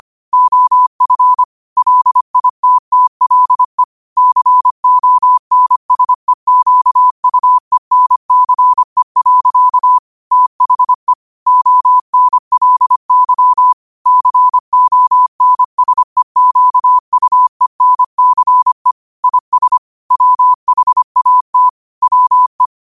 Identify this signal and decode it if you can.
Morse code, heard as a keyed tone, about 25 wpm, 1000 Hz. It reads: OF LITTLE CONSEQUENCE. THE ONLY CONSEQUENCE IS WHAT WE
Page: New quotes every day in morse code at 25 Words per minute.